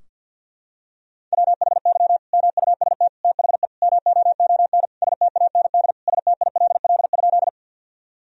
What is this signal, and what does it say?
Morse code, heard as a keyed tone, about 50 wpm, 700 Hz. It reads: WHY MUST THE MOON STAND STILL?
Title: Sentences from Top 1000 Words - Review Part 3 - 50wpm